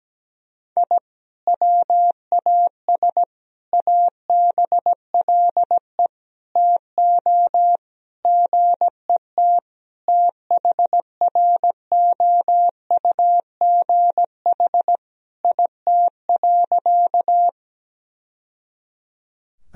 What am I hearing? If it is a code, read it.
Morse code, heard as a keyed tone, about 17 wpm, 700 Hz. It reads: I WAS ABLE TO GET THROUGH IT.
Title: Sentences from Top 1000 Words - Review Part 2 - 17wpm